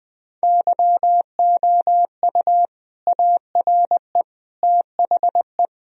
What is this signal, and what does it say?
Morse code, heard as a keyed tone, about 20 wpm, 700 Hz. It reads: YOU ARE THE
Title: Sentences from Top 1000 Words - Review Part 2 - 20wpm